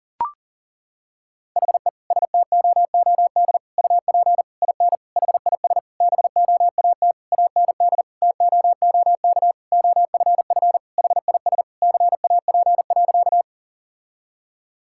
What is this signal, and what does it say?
Morse code, heard as a keyed tone, about 37 wpm, 700 Hz. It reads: HE STOOD UP IN HIS BOAT AND TOOK OFF HIS CAP.
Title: Sentences from Top 1600 Words 40wpm